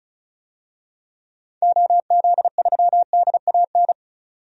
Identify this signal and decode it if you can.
Morse code, heard as a keyed tone, about 34 wpm, 700 Hz. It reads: OZ3DAN
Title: DXCC-100 Call Signs - 6 Characters - 35wpm